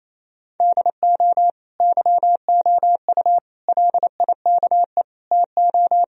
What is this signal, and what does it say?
Morse code, heard as a keyed tone, about 28 wpm, 700 Hz. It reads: DO YOU LIKE TO
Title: Sentences from Top 1000 Words - Review Part 2 - 28wpm